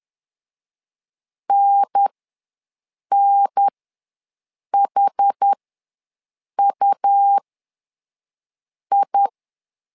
Morse code, heard as a keyed tone, about 10 wpm, spaced as for 5 wpm, 800 Hz. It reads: NNHUI
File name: 2 UHDN ws 10wpm